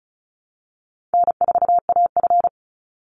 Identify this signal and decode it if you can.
Morse code, heard as a keyed tone, about 34 wpm, 700 Hz. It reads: N4AF
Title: Easy Call Signs Repeated 2x - 35wpm